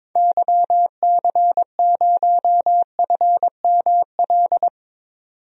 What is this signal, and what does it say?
Morse code, heard as a keyed tone, about 22 wpm, 700 Hz. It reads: YC0FML
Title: DXCC-100 Call Signs - 6 Characters - 22wpm